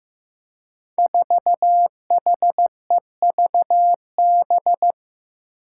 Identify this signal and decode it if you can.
Morse code, heard as a keyed tone, about 15 wpm, 700 Hz. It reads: 4HEVB
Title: Five Challenging Letter-Number Combinations 15wpm